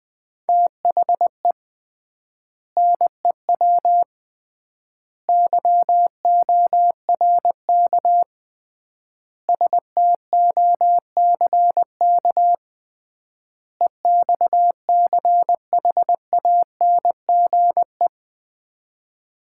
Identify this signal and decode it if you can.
Morse code, heard as a keyed tone, about 20 wpm, 700 Hz. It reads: THE NEW YORK STOCK EXCHANGE
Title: 3x Word-Spacing - Sets of 5 Words - 20wpm